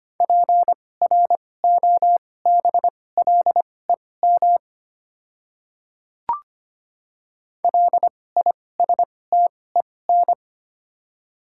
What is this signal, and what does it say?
Morse code, heard as a keyed tone, about 25 wpm, spaced as for 18 wpm, 700 Hz. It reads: PROBLEM LISTEN
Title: 2x Character Spacing - Top 500 Words - 25wpm